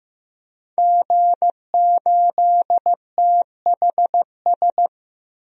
Morse code, heard as a keyed tone, about 15 wpm, 700 Hz. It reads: G8THS
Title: DXCC-100 Call Signs - 5 Characters - 15wpm